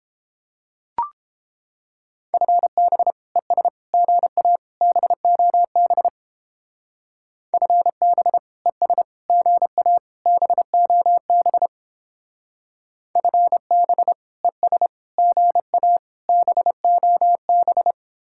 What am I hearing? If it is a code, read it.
Morse code, heard as a keyed tone, about 31 wpm, 700 Hz. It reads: FB ES GA BOB FB ES GA BOB FB ES GA BOB